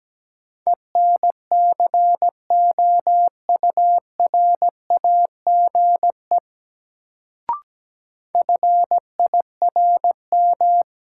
Morse code, heard as a keyed tone, about 17 wpm, 700 Hz. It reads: ENCOURAGE FIRM
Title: Sentences from Top 1400 Words - New Word Drill - 17wpm